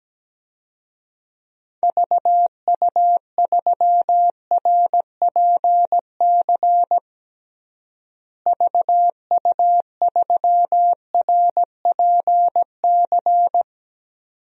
Morse code, heard as a keyed tone, about 17 wpm, 700 Hz. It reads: VU3RPC VU3RPC
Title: DXCC-100 Call Signs - 6 Characters - Repeated 2x - 17wpm